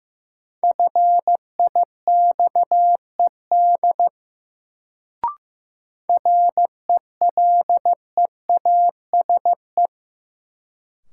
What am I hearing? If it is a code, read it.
Morse code, heard as a keyed tone, about 15 wpm, 700 Hz. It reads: FIXED RELEASE
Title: Sentences from Top 1100 Words - New Word Drill Alt - 15wpm